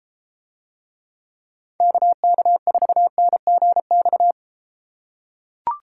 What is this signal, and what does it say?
Morse code, heard as a keyed tone, about 34 wpm, 700 Hz. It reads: KK4NGX